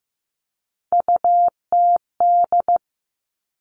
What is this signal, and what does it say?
Morse code, heard as a keyed tone, about 15 wpm, 700 Hz. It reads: UTD
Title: Three Challenging Letter-Number Combinations 15wpm